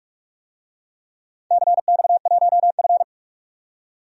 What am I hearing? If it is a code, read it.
Morse code, heard as a keyed tone, about 44 wpm, 700 Hz. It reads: CX1F
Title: DXCC-100 Call Signs - 4 Characters - 45wpm